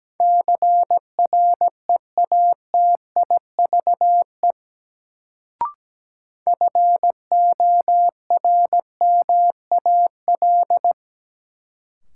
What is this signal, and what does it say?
Morse code, heard as a keyed tone, about 17 wpm, 700 Hz. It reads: CREATIVE FORMAL
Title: Sentences from Top 1300 Words - New Word Drill Alt - 17wpm